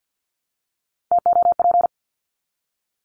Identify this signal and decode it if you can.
Morse code, heard as a keyed tone, about 50 wpm, 700 Hz. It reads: TOP
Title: Top 500 Words 50wpm